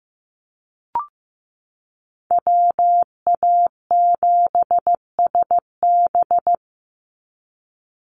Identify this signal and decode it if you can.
Morse code, heard as a keyed tone, about 15 wpm, 700 Hz. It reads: WA7SB
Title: NA SOTA Chaser Callsigns 15wpm